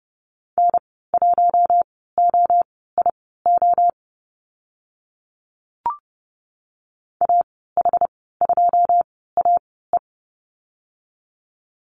Morse code, heard as a keyed tone, about 30 wpm, spaced as for 17 wpm, 700 Hz. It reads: N1OIO AH2AE
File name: Lesson 14g - Number 2 - Call Signs 30wpm